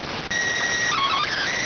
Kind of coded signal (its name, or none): SSTV (slow-scan television)